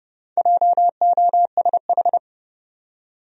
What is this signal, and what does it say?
Morse code, heard as a keyed tone, about 30 wpm, 700 Hz. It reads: JOSH
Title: USA Common Names - 30wpm